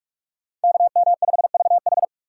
Morse code, heard as a keyed tone, about 46 wpm, 700 Hz. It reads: KM5VH
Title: USA Call Signs - 5 Characters - 2x Character Spacing - 45wpm